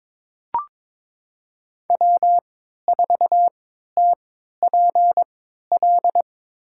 Morse code, heard as a keyed tone, about 22 wpm, spaced as for 13 wpm, 700 Hz. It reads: W4TPL
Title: USA Call Signs - 5 Characters - 3x Character Spacing - 22wpm